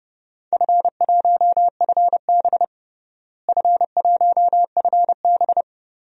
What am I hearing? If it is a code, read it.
Morse code, heard as a keyed tone, about 30 wpm, 700 Hz. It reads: F1FB F1FB
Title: DXCC-100 Call Signs - 4 Characters - Repeated 3x - 30wpm